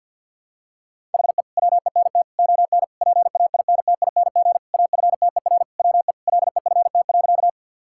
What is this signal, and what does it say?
Morse code, heard as a keyed tone, about 46 wpm, 700 Hz. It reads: HE WENT ON PAINTING AFTER WE LEFT.
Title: Sentences from Top 1000 Words - Review Part 6 - 50wpm